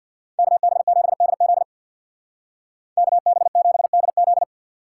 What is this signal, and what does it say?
Morse code, heard as a keyed tone, about 64 wpm, 700 Hz. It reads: KB7DZ KB7DZ